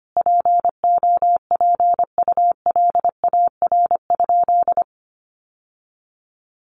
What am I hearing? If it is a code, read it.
Morse code, heard as a keyed tone, about 25 wpm, 700 Hz. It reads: POPULAR?
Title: Sentences from Top 1200 Words 25wpm